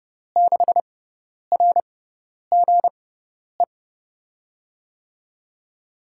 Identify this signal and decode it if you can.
Morse code, heard as a keyed tone, about 30 wpm, spaced as for 10 wpm, 700 Hz. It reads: 6RGE
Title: USA Call Signs - 6x Character Spacing - 30wpm